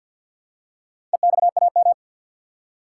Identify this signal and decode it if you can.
Morse code, heard as a keyed tone, about 49 wpm, 700 Hz. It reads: EXAM
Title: Four Letter Words - 50wpm